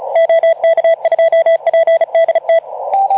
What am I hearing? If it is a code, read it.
Morse code, heard as a keyed tone, about 34 wpm, 660 Hz. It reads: OK2PDT
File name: Archiv / CUC181 / OK2ZC CUC 181